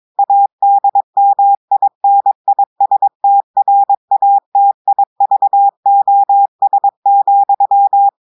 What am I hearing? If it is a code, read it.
Morse code, heard as a keyed tone, about 22 wpm, 800 Hz. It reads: ADMINISTRATIVOS,
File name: audios / portugais / portugais texte en clair 2 vitesse 22 mots minutes